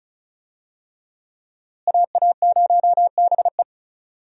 Morse code, heard as a keyed tone, about 36 wpm, 700 Hz. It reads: AA0BE
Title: USA Call Signs - 5 Characters - Repeated 3x - 35wpm